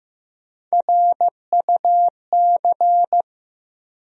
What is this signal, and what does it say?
Morse code, heard as a keyed tone, about 15 wpm, 700 Hz. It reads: RUC
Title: USA Call Sign Postfixes - 3 Letters 15wpm